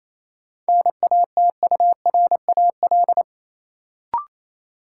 Morse code, heard as a keyed tone, about 28 wpm, 700 Hz. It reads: NATURAL
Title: Sentences from Top 700 Words - New Word Drill - 28wpm